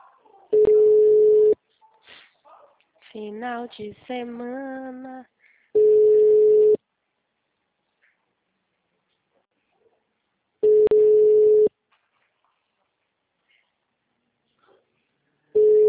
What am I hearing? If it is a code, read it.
Ringback tone (Europe, 425 Hz), with speech.